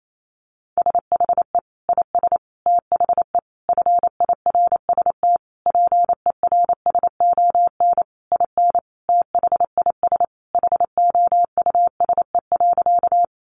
Morse code, heard as a keyed tone, about 28 wpm, 700 Hz. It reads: SHE IS THE FIRST PERSON IN THIS HOUSE.
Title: Sentences from Top 400 Words 28wpm